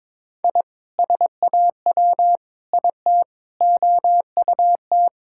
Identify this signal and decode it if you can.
Morse code, heard as a keyed tone, about 22 wpm, 700 Hz. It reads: I SAW IT OUT
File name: Sentences with 7-Word Phrases - 22wpm